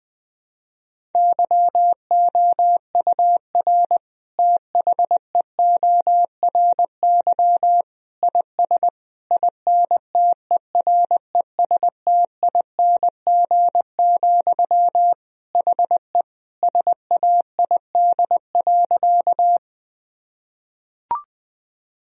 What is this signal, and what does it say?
Morse code, heard as a keyed tone, about 20 wpm, 700 Hz. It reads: YOUR THEORY IS INTERESTING, HE SAID.
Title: Sentences from Top 1000 Words - Review Part 3 - 20wpm